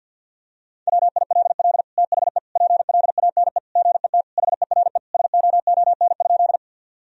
Morse code, heard as a keyed tone, about 50 wpm, 700 Hz. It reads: WILL THE PLANE GET HERE SOON?
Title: Sentences from Top 1000 Words - Review Part 3 - 50wpm